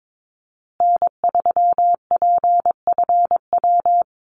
Morse code, heard as a keyed tone, about 22 wpm, 700 Hz. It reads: N3PFW
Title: USA Call Signs Repeated 3x 22wpm